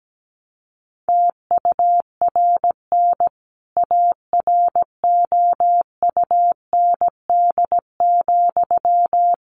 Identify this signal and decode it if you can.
Morse code, heard as a keyed tone, about 17 wpm, 700 Hz. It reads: TURN AROUND,
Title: A Sound Of Thunder 17wpm